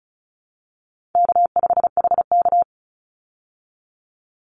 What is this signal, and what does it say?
Morse code, heard as a keyed tone, about 34 wpm, 700 Hz. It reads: K5HK
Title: Easy Call Signs Repeated 2x - 35wpm